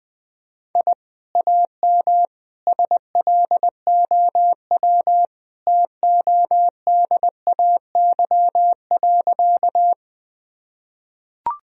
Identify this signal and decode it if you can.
Morse code, heard as a keyed tone, about 20 wpm, 700 Hz. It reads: I AM SLOW TODAY.
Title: Sentences from Top 1000 Words - Review Part 2 - 20wpm